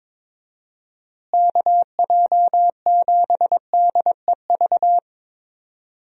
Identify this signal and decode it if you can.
Morse code, heard as a keyed tone, about 21 wpm, 700 Hz. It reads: KJ7DEV
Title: USA Call Signs - 6 Characters - Repeated 2x - 22wpm